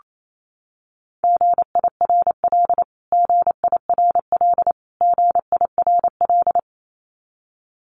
Morse code, heard as a keyed tone, about 28 wpm, 700 Hz. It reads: GIRL GIRL GIRL
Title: Top 500 Words Repeated 28wpm